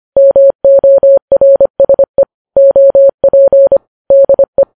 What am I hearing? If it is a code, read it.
Morse code, heard as a keyed tone, about 25 wpm, 550 Hz. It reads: MORSE OP DE